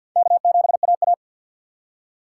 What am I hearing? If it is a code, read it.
Morse code, heard as a keyed tone, about 66 wpm, spaced as for 57 wpm, 700 Hz. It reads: K7AA
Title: Speed Racing - USA 1x2 Call Signs 50wpm